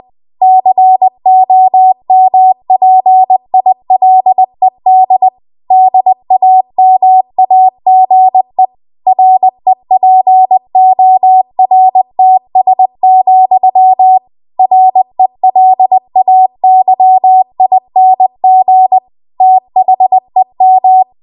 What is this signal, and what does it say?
Morse code, heard as a keyed tone, about 20 wpm, 750 Hz. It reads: COMPILED DAMAGE REPORTS, RELAYING THEM
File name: Archive / 20 WPM / 20WPM